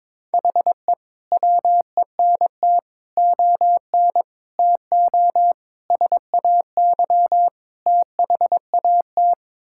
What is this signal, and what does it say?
Morse code, heard as a keyed tone, about 22 wpm, 700 Hz. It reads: HE WENT ON TO SAY THAT
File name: practice / 22wpm / Sets of 6 Words - 22wpm